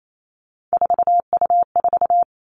Morse code, heard as a keyed tone, about 28 wpm, 700 Hz. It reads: 4U4
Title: Three Challenging Letter-Number Combinations 28wpm